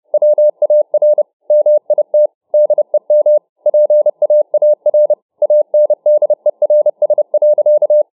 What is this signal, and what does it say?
Morse code, heard as a keyed tone, about 29 wpm, 600 Hz. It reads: WAR MIT DEM PAAR ANDERS.